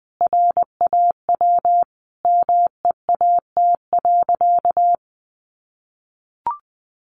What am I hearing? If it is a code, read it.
Morse code, heard as a keyed tone, about 19 wpm, 700 Hz. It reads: RAW MEAT.
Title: Sentences from Top 1500 Words 20wpm